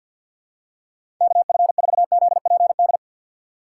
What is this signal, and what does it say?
Morse code, heard as a keyed tone, about 47 wpm, 700 Hz. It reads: KF4ZPD
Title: USA Call Signs - 3x Character Spacing - 50wpm